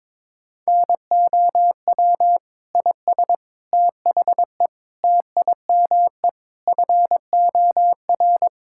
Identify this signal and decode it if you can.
Morse code, heard as a keyed tone, about 22 wpm, 700 Hz. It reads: NOW IS THE TIME FOR
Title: Pangram Challenge v2 - 22wpm